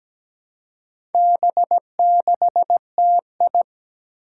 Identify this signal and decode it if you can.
Morse code, heard as a keyed tone, about 17 wpm, 700 Hz. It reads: B6TI
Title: Four Challenging Letter-Number Combinations 17wpm